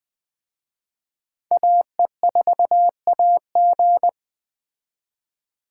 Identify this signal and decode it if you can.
Morse code, heard as a keyed tone, about 21 wpm, 700 Hz. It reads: AE4AG
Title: Speed Racing - USA Call Signs 17wpm